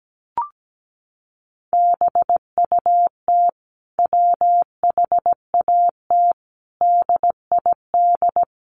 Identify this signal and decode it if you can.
Morse code, heard as a keyed tone, about 17 wpm, 700 Hz. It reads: BUT WHAT DID
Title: Sentences from Top 100 Words 17wpm